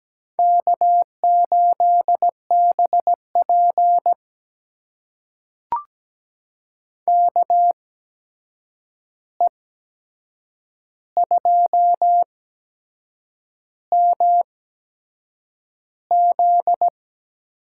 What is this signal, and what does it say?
Morse code, heard as a keyed tone, about 17 wpm, spaced as for 7 wpm, 700 Hz. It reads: K8BP KE2MZ
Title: USA Call Signs - 8x Character Spacing - 17wpm